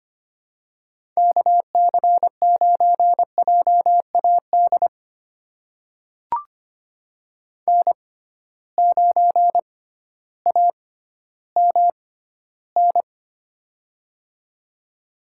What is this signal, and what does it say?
Morse code, heard as a keyed tone, about 25 wpm, spaced as for 13 wpm, 700 Hz. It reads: KC9JAD N9AMN
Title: USA Call Signs - 6x Character Spacing - 25wpm